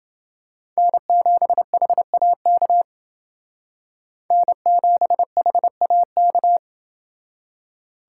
Morse code, heard as a keyed tone, about 29 wpm, 700 Hz. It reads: N7HAK N7HAK